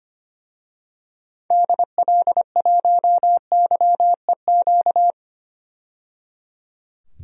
Morse code, heard as a keyed tone, about 25 wpm, 700 Hz. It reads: DL1YEQ
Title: DXCC-100 Call Signs - 6 Characters - 25wpm